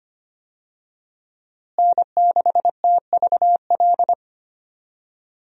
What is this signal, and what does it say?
Morse code, heard as a keyed tone, about 25 wpm, 700 Hz. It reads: N6TVL
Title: USA Call Signs - 5 Characters - Repeated 2x - 25wpm